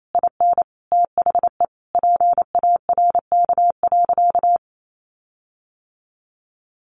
Morse code, heard as a keyed tone, about 28 wpm, 700 Hz. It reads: IN THE PARK.
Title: Sentences from Top 1300 Words 28wpm